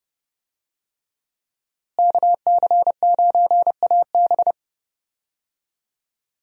Morse code, heard as a keyed tone, about 32 wpm, 700 Hz. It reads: KC9AB
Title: USA Call Signs - 5 Characters - Repeated 3x - 30wpm